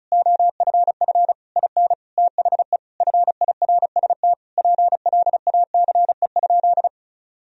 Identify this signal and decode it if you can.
Morse code, heard as a keyed tone, about 34 wpm, 700 Hz. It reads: OFF IN THE FIRST PLACE?
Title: Sentences from Top 200 Words and Abbreviations 35wpm